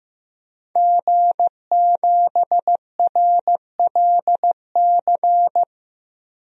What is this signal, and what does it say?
Morse code, heard as a keyed tone, about 15 wpm, 700 Hz. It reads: G7RLC
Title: DXCC-100 Call Signs - 5 Characters - Repeated 2x - 15wpm